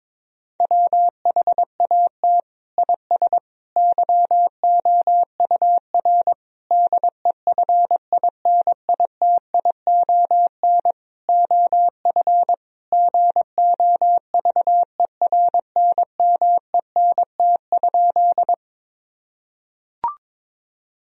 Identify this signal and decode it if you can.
Morse code, heard as a keyed tone, about 22 wpm, 700 Hz. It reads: WHAT IS YOUR DEFINITION OF GOVERNMENT?
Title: Sentences from Top 1000 Words - Review Part 4 - 22wpm